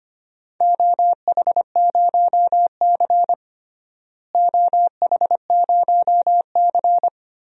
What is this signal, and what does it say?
Morse code, heard as a keyed tone, about 25 wpm, 700 Hz. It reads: OH0C OH0C
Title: DXCC-100 Call Signs - 4 Characters - Repeated 2x - 25wpm